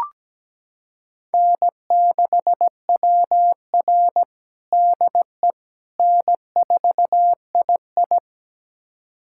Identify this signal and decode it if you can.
Morse code, heard as a keyed tone, about 17 wpm, 700 Hz. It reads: N6WR DE N4II
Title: USA Callsign DE Callsign - 4 Characters - 17wpm